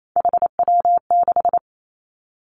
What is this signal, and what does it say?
Morse code, heard as a keyed tone, about 28 wpm, 690 Hz. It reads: HW6
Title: Three Challenging Letter-Number Combinations 28wpm